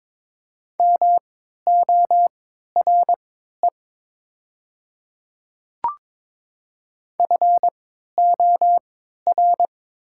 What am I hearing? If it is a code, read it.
Morse code, heard as a keyed tone, about 22 wpm, spaced as for 12 wpm, 700 Hz. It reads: MORE FOR
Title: 3x Character Spacing - Top 100 Words - 22wpm